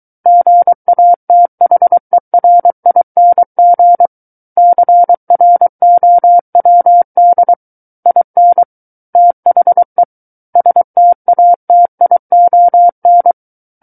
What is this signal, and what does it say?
Morse code, heard as a keyed tone, about 23 wpm, 700 Hz. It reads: GATHERING CROWD IN THE STATION
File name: Book One / 14-Chapter 14 - 23 WPM